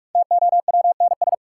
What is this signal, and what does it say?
Morse code, heard as a keyed tone, about 46 wpm, 700 Hz. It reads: TOWNS
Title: Sentences from Top 1000 Words - Review Part 6 - New Word Drill - 45wpm